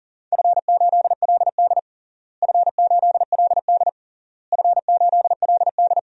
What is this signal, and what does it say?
Morse code, heard as a keyed tone, about 40 wpm, 700 Hz. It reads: F8LD F8LD F8LD
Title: DXCC-100 Call Signs - 4 Characters - Repeated 3x - 40wpm